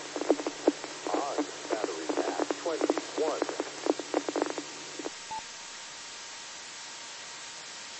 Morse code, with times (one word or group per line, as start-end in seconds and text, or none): none